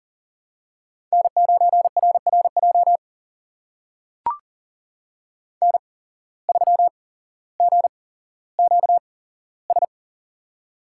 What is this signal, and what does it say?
Morse code, heard as a keyed tone, about 40 wpm, spaced as for 17 wpm, 700 Hz. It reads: N9RRJ N3GQS
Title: USA Call Signs - 5 Characters - 8x Character Spacing - 40wpm